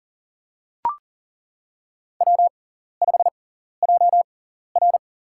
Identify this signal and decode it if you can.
Morse code, heard as a keyed tone, about 40 wpm, spaced as for 14 wpm, 700 Hz. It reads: W5JR
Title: USA Call Signs - 4 Characters - 6x Character Spacing - 40wpm